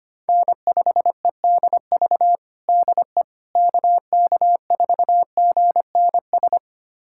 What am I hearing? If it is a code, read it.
Morse code, heard as a keyed tone, about 25 wpm, 700 Hz. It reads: N5EDV DE KK4GNS
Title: USA Callsign DE Callsign 25wpm